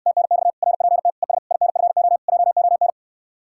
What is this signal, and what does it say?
Morse code, heard as a keyed tone, about 69 wpm, 700 Hz. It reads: TNX RPT ES INFO JON